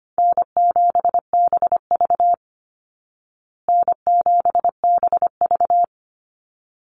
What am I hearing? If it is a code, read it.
Morse code, heard as a keyed tone, about 25 wpm, 700 Hz. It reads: N7BV N7BV
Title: Easy Call Signs Repeated 2x - 25wpm